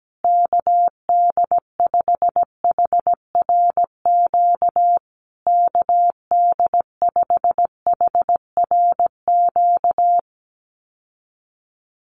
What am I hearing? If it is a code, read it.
Morse code, heard as a keyed tone, about 17 wpm, 700 Hz. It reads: KD5HRQ KD5HRQ
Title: USA Call Signs Repeated 3x 17wpm